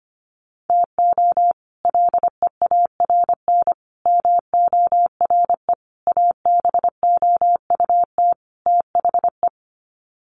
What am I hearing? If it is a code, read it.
Morse code, heard as a keyed tone, about 25 wpm, 700 Hz. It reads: TO LEARN MORE ABOUT THE
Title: Sets of 5 Words Encore2 25wpm